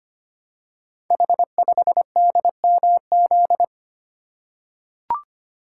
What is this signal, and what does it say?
Morse code, heard as a keyed tone, about 25 wpm, 700 Hz. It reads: H5DMZ
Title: Code Groups - 5 Letter:Numbers - 25wpm